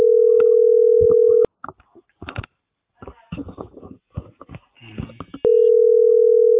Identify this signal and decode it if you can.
Ringback tone (North America), 440 + 480 Hz.
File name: outgoing recordings